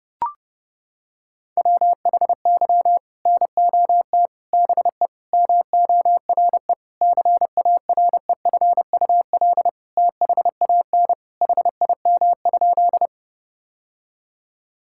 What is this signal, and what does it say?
Morse code, heard as a keyed tone, about 32 wpm, 700 Hz. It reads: WHY NOT BE MORE CAREFUL THAN HIM?
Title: Sentences from Top 1600 Words 30wpm